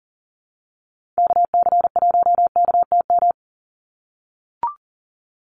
Morse code, heard as a keyed tone, about 40 wpm, 700 Hz. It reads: KC1KTM
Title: USA Call Signs 40wpm